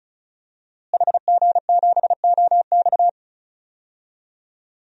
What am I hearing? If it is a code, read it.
Morse code, heard as a keyed tone, about 36 wpm, 700 Hz. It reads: HG7OX
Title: DXCC-100 Call Signs - 5 Characters - Repeated 2x - 35wpm